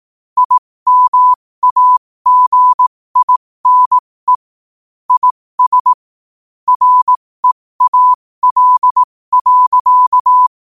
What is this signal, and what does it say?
Morse code, heard as a keyed tone, about 18 wpm, spaced as for 15 wpm, 1000 Hz. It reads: IMAGINE IS REAL.
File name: qotdfiles / 15wpm